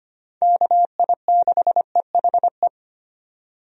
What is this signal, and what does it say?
Morse code, heard as a keyed tone, about 25 wpm, 700 Hz. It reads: KI6EHE
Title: USA Call Signs - 6 Characters - 25wpm